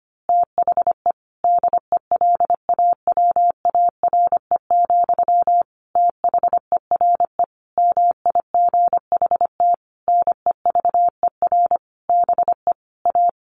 Morse code, heard as a keyed tone, about 25 wpm, 700 Hz. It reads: THE DELAWARE, THERE MIGHT NEVER BE A
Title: A Sound Of Thunder 25wpm